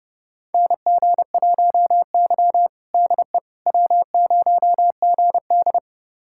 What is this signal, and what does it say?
Morse code, heard as a keyed tone, about 30 wpm, 700 Hz. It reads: NG1Y DE W0GD
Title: USA Callsign DE Callsign - 4 Characters - 30wpm